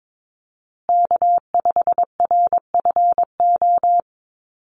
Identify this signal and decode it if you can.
Morse code, heard as a keyed tone, about 22 wpm, 700 Hz. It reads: K5RFO
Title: USA Call Signs Repeated 2x 22wpm